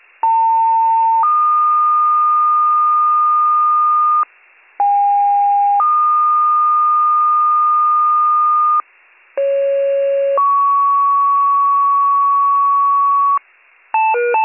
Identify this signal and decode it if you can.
Two-tone page, heard as a one-second tone, then a three-second tone, about 900 then 1220 Hz, about 810 then 1220 Hz, about 550 then 1120 Hz. It is heard from 0.2 to 13.4 s.